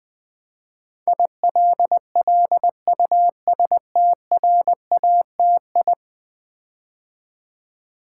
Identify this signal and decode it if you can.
Morse code, heard as a keyed tone, about 20 wpm, 700 Hz. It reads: ILLUSTRATI
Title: Build A Word - 5 Letters At A Time - 12-Letter Words - 20wpm